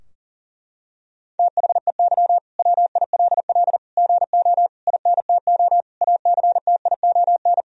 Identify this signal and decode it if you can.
Morse code, heard as a keyed tone, about 40 wpm, 700 Hz. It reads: THEY WILL GO INTO ACTION
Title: Sentences from Top 1000 Words - Review Part 4 - 40wpm